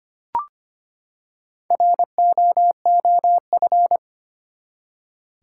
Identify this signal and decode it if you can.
Morse code, heard as a keyed tone, about 25 wpm, 700 Hz. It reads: ROOF
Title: Four Letter Words - 25wpm